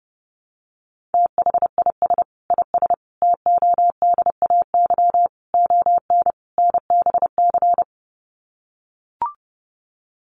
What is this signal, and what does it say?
Morse code, heard as a keyed tone, about 30 wpm, 700 Hz. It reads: THIS IS TODAY ON NBC
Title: Sets of 5 Words Encore 30wpm